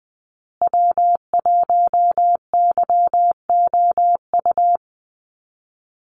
Morse code, heard as a keyed tone, about 20 wpm, 700 Hz. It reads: W1YOU
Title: USA Call Signs Repeated 3x 20wpm